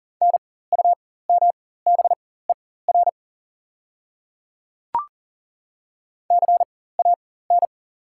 Morse code, heard as a keyed tone, about 40 wpm, spaced as for 19 wpm, 690 Hz. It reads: NUMBER CAN
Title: 4x Character Spacing - Top 200 Words - 40wpm